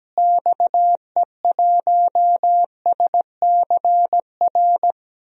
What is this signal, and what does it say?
Morse code, heard as a keyed tone, about 17 wpm, 700 Hz. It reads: XE1SCR
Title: DXCC-100 Call Signs - 6 Characters - Repeated 3x - 17wpm